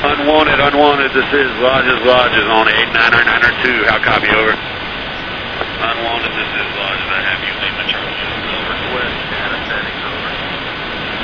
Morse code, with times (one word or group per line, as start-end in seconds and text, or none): none